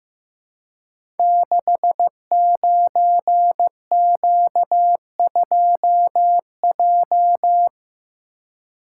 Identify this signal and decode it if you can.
Morse code, heard as a keyed tone, about 15 wpm, 700 Hz. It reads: 69Q2J
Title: Code Groups - 5 Letter:Numbers - 15wpm